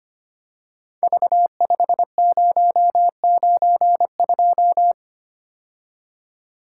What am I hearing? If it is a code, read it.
Morse code, heard as a keyed tone, about 25 wpm, 700 Hz. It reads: V5092
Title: Code Groups - 5 Letter:Numbers - 25wpm